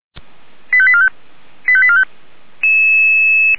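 Five-tone selective call (ZVEI), heard as quick runs of five tones, tones of about 70 ms. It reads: 85845 85845
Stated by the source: ZVEI five-tone selective call 89484 received on SDR.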